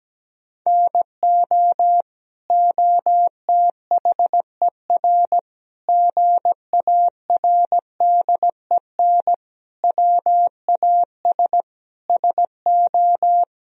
Morse code, heard as a keyed tone, about 17 wpm, 700 Hz. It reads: NO OTHER GARDEN WAS SO
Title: Sentences from Top 1000 Words - Review Part 4 - 17wpm